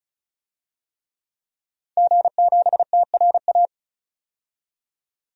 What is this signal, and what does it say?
Morse code, heard as a keyed tone, about 34 wpm, 700 Hz. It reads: G7TRA
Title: DXCC-100 Call Signs - 5 Characters - 35wpm